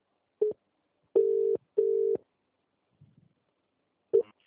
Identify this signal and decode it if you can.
Ringback tone (UK), 400 + 450 Hz.